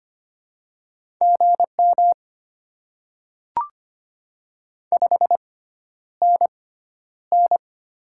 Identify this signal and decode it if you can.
Morse code, heard as a keyed tone, about 25 wpm, spaced as for 11 wpm, 690 Hz. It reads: GM 5NN
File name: Top 20 QSO Elements - 6x Character Spacing - 25wpm